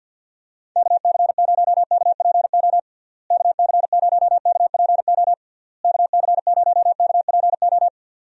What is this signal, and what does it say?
Morse code, heard as a keyed tone, about 52 wpm, 700 Hz. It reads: KC0KPO KC0KPO KC0KPO
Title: USA Call Signs - 6 Characters - Repeated 3x - 50wpm